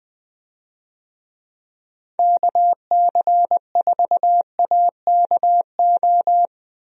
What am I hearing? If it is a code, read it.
Morse code, heard as a keyed tone, about 20 wpm, 700 Hz. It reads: KC4AKO
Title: USA Call Signs - 6 Characters - 20wpm